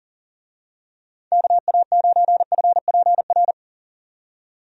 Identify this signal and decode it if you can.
Morse code, heard as a keyed tone, about 38 wpm, 700 Hz. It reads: KA9FPR
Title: USA Call Signs - 6 Characters - 40wpm